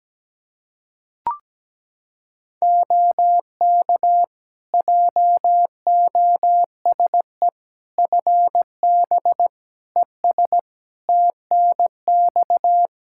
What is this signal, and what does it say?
Morse code, heard as a keyed tone, about 17 wpm, 700 Hz. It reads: OK JOSE FB ES TNX
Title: K1USN SST QSO Practice 17wpm